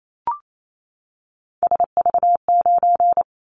Morse code, heard as a keyed tone, about 28 wpm, 700 Hz. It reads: SV9
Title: Callsign Prefixes for Top 100 DXCC - 28wpm